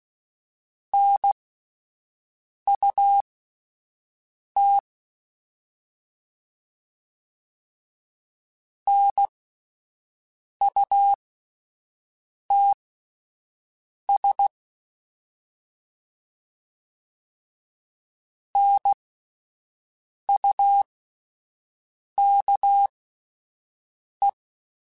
Morse code, heard as a keyed tone, about 16 wpm, spaced as for 5 wpm, 800 Hz. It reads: NUT NUTS NUKE